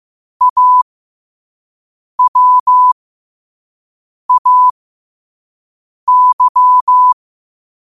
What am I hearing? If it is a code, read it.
Morse code, heard as a keyed tone, about 15 wpm, spaced as for 5 wpm, 1000 Hz. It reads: AWAY
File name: qotdfiles / 5wpm